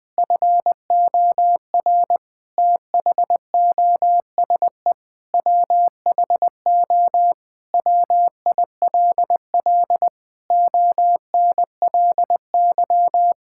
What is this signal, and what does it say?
Morse code, heard as a keyed tone, about 20 wpm, 700 Hz. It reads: FOR THOSE WHO WILL ONLY
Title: Sentences from Top 1000 Words - Review Part 6 - 20wpm